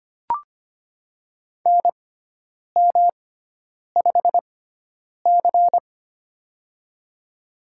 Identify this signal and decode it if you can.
Morse code, heard as a keyed tone, about 25 wpm, spaced as for 9 wpm, 700 Hz. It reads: NM5C
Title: USA Call Signs - 4 Characters - 6x Character Spacing - 25wpm